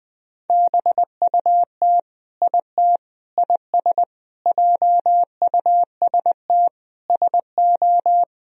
Morse code, heard as a keyed tone, about 20 wpm, 700 Hz. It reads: BUT IT IS JUST SO
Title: Sentences from Top 200 Words and Abbreviations 20wpm